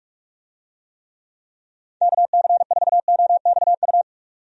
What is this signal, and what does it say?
Morse code, heard as a keyed tone, about 46 wpm, 700 Hz. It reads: KC4OXU
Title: USA Call Signs - 6 Characters - 45wpm